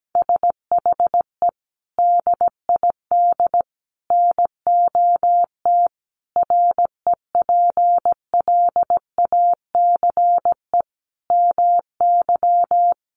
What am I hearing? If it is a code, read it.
Morse code, heard as a keyed tone, about 17 wpm, 700 Hz. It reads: SHE DID NOT REPLACE MY
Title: Sentences from Top 1400 Words 17wpm